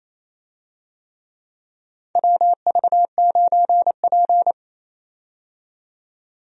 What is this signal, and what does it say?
Morse code, heard as a keyed tone, about 28 wpm, 700 Hz. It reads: WV9P
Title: USA Call Signs - 4 Characters - 28wpm